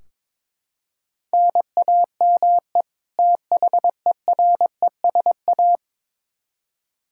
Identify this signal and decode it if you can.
Morse code, heard as a keyed tone, about 22 wpm, 700 Hz. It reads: NAME THERESA
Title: K1USN SST QSO Practice 22wpm